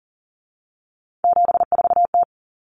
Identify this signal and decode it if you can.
Morse code, heard as a keyed tone, about 40 wpm, 700 Hz. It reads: 74T
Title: Common Cut Numbers - 3-Digit 40wpm